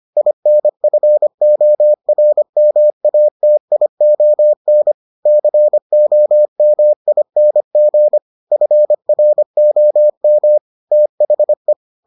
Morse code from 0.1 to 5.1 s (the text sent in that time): INFORMATION